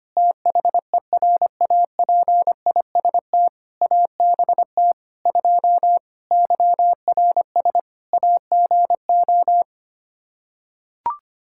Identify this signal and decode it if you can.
Morse code, heard as a keyed tone, about 25 wpm, 700 Hz. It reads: THERAPIST ABT 2 YRS AGO
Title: K1USN SST QSO Practice 25wpm